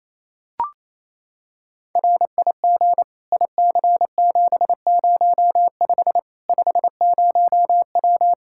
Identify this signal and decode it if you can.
Morse code, heard as a keyed tone, about 28 wpm, 700 Hz. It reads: RIG IC705 50W
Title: Practice QSO - Exchange 2 - Info - 28wpm